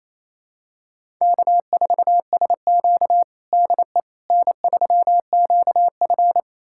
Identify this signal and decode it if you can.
Morse code, heard as a keyed tone, about 28 wpm, 700 Hz. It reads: K4SQ DE N3QF
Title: USA Callsign DE Callsign - 4 Characters - 28wpm